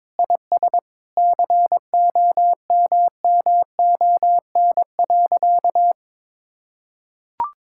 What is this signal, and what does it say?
Morse code, heard as a keyed tone, about 22 wpm, 700 Hz. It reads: IS COMMON.
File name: practice / 22wpm / Sentences with 4-Word Phrases - 22wpm